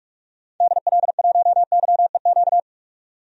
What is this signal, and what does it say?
Morse code, heard as a keyed tone, about 46 wpm, 700 Hz. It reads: DL1YEQ
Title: DXCC-100 Call Signs - 6 Characters - 45wpm